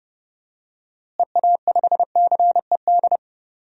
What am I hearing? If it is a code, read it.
Morse code, heard as a keyed tone, about 30 wpm, 700 Hz. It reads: EA5CED
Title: DXCC-100 Call Signs - 6 Characters - Repeated 3x - 30wpm